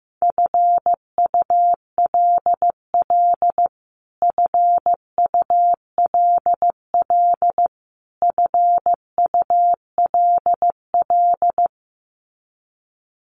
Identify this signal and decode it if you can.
Morse code, heard as a keyed tone, about 15 wpm, 700 Hz. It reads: FULL FULL FULL
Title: Top 500 Words Repeated 15wpm